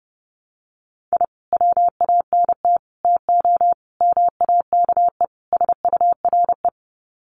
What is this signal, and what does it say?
Morse code, heard as a keyed tone, about 30 wpm, 700 Hz. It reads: I WANT TO MAKE SURE
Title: Sets of 5 Words Encore2 30wpm